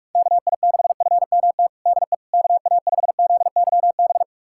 Morse code, heard as a keyed tone, about 41 wpm, 690 Hz. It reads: KI6FMT DE KA5ZYB
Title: USA Callsign DE Callsign - 6 Characters - 45wpm